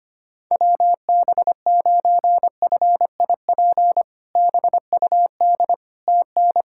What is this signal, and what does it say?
Morse code, heard as a keyed tone, about 25 wpm, 700 Hz. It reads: WB9FIP BUD TN
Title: K1USN SST QSO Practice - 25wpm